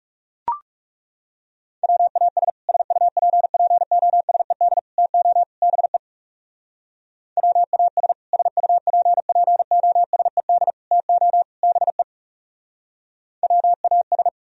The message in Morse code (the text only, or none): WAS SUPPOSED TO BE WAS SUPPOSED TO BE WAS